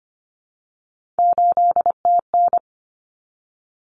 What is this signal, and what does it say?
Morse code, heard as a keyed tone, about 25 wpm, 700 Hz. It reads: 8TN
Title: Common Cut Numbers - 3-Digit 25wpm